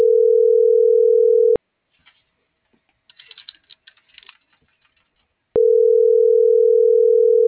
Ringback tone (North America), 440 + 480 Hz.